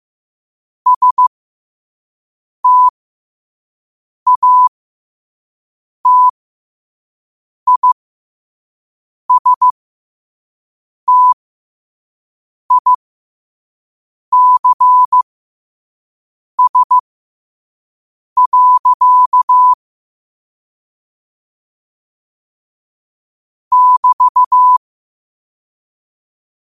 Morse code, heard as a keyed tone, about 15 wpm, spaced as for 5 wpm, 1000 Hz. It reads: STATISTICS. <BT>
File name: qotdfiles / 5wpm